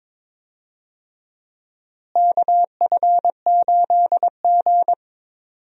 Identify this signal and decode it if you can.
Morse code, heard as a keyed tone, about 22 wpm, 700 Hz. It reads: KF8G
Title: USA Call Signs - 4 Characters - Repeated 2x - 22wpm